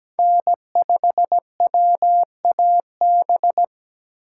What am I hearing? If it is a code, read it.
Morse code, heard as a keyed tone, about 17 wpm, 700 Hz. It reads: N5WAB
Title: USA Call Signs - 5 Characters - 2x Character Spacing - 17wpm